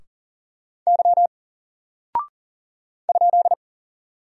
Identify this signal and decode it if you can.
Morse code, heard as a keyed tone, about 39 wpm, 700 Hz. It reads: Y ?
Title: Single Character (Letter:Number:Punctuation:Prosign) 40wpm